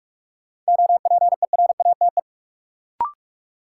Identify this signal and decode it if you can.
Morse code, heard as a keyed tone, about 43 wpm, 700 Hz. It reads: OPERATE
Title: Sentences from Top 1500 Words - New Word Drill Alt - 45wpm